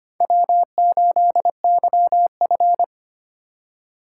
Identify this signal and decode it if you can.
Morse code, heard as a keyed tone, about 25 wpm, 700 Hz. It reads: W8YF
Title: USA Call Signs - 4 Characters - 3x Character Spacing - 25wpm